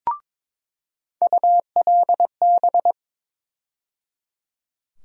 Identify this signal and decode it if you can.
Morse code, heard as a keyed tone, about 22 wpm, 700 Hz. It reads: ULB
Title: USA Call Sign Postfixes - 3 Letters 22wpm